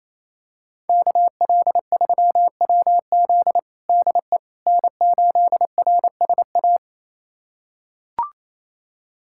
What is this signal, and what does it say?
Morse code, heard as a keyed tone, about 28 wpm, 700 Hz. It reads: KL3WZ DE N8RSA
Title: USA Callsign DE Callsign - 5 Characters - 28wpm